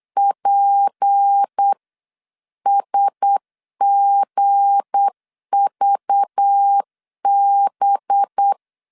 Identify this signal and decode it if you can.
Morse code, heard as a keyed tone, about 8 wpm, 800 Hz. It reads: PSGVB
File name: CW / DET / 1 9 1500C S C 8wpm 5653